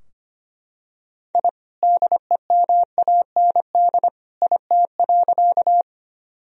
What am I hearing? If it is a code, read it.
Morse code, heard as a keyed tone, about 25 wpm, 700 Hz. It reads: I DEMAND IT.
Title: Sentences from Top 1000 Words - Review Part 4 - 25wpm